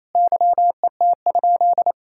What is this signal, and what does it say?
Morse code, heard as a keyed tone, about 28 wpm, 700 Hz. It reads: YET?
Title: Sentences from Top 1000 Words - Review Part 5 - 28wpm